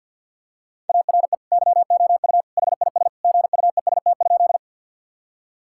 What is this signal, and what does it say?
Morse code, heard as a keyed tone, about 50 wpm, 700 Hz. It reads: ARE YOU HIS GUEST?
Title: Sentences from Top 1600 Words 50wpm